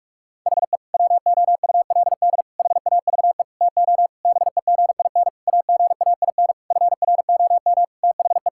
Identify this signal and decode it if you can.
Morse code, heard as a keyed tone, about 46 wpm, 700 Hz. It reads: HE WOULD HAVE TO BEGIN AGAIN FROM THE
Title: Sentences from Top 1000 Words - Review Part 5 - 45wpm